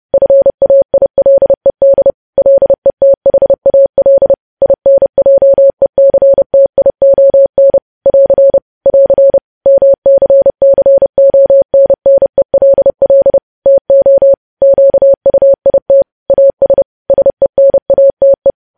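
Morse code, heard as a keyed tone, about 30 wpm, 550 Hz. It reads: FAILED LETHAL INJECTION <AR> <AR> MCCONNELL TO QUIT AS SENATE
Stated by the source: News headlines sent in Morse at 30 wpm, preceded by VVV.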